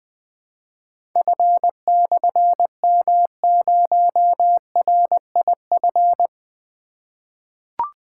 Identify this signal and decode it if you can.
Morse code, heard as a keyed tone, about 20 wpm, 700 Hz. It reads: F/M0RIF
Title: Difficult Call Signs 20wpm